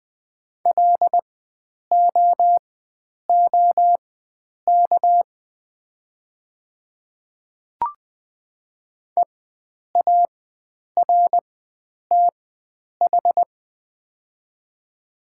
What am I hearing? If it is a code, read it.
Morse code, heard as a keyed tone, about 20 wpm, spaced as for 9 wpm, 700 Hz. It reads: LOOK EARTH
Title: 4x Character Spacing - Top 500 Words - 20wpm